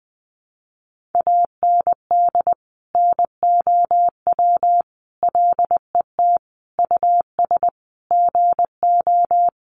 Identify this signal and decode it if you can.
Morse code, heard as a keyed tone, about 20 wpm, 700 Hz. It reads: AND NOW LET US GO
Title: Sentences from Top 300 Words 20wpm